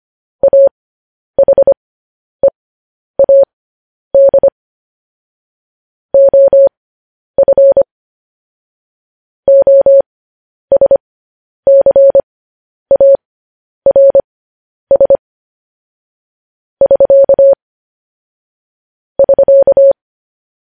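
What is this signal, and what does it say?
Morse code, heard as a keyed tone, about 25 wpm, spaced as for 10 wpm, 550 Hz. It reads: AHEAD OF OSCARS <SK> <SK>
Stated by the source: Morse code news headlines practice transmission.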